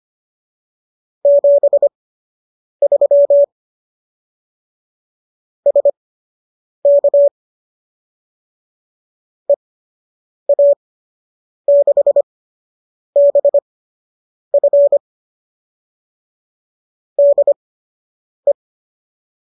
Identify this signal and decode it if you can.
Morse code, heard as a keyed tone, about 25 wpm, spaced as for 8 wpm, 570 Hz. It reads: 73 SK EA6BF DE